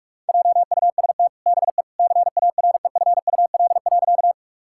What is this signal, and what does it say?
Morse code, heard as a keyed tone, about 46 wpm, 700 Hz. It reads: JUST BE CAREFUL.
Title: Sentences from Top 1600 Words 45wpm